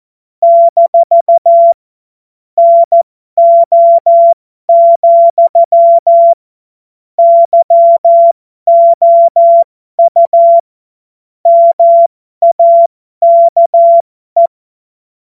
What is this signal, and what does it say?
Morse code, heard as a keyed tone, about 14 wpm, spaced as for 12 wpm, 680 Hz. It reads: - NO, YOU MAKE
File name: Morsekurs / Witze